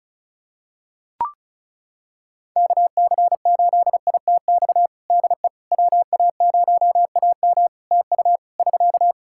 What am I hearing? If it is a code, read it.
Morse code, heard as a keyed tone, about 36 wpm, 700 Hz. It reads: KC8ITX DE WA0AM TU <SK>
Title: Practice QSO - Exchange 4 - Ending - 35wpm